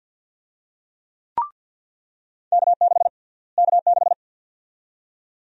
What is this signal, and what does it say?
Morse code, heard as a keyed tone, about 46 wpm, 700 Hz. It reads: K6 K6
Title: USA Call Sign Prefixes - Repeated 2x - 50wpm